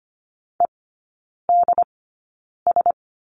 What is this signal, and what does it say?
Morse code, heard as a keyed tone, about 25 wpm, spaced as for 9 wpm, 700 Hz. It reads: EDS
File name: course / 25:10wpm / 093 Lesson 16e - Letter c - Three Characters 25wpm